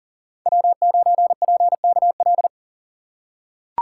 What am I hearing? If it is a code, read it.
Morse code, heard as a keyed tone, about 39 wpm, 700 Hz. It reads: W9PKL
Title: USA Call Signs - 5 Characters - 2x Character Spacing - 40wpm